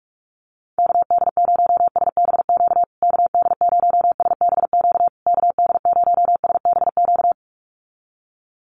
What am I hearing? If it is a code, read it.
Morse code, heard as a keyed tone, about 46 wpm, 700 Hz. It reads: KD0SBQ KD0SBQ KD0SBQ
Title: USA Call Signs Repeated 3x 45wpm